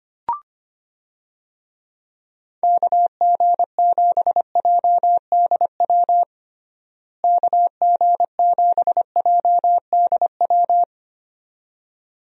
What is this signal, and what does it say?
Morse code, heard as a keyed tone, about 25 wpm, 700 Hz. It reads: KG7JDW KG7JDW
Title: USA Call Signs - 6 Characters - Repeated 2x - 25wpm